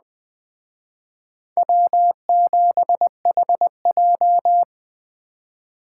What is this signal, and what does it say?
Morse code, heard as a keyed tone, about 21 wpm, 700 Hz. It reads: W7HJ
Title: Speed Racing - USA Call Signs 17wpm